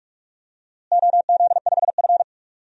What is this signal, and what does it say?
Morse code, heard as a keyed tone, about 46 wpm, 700 Hz. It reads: OZ5F
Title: DXCC-100 Call Signs - 4 Characters - Repeated 3x - 45wpm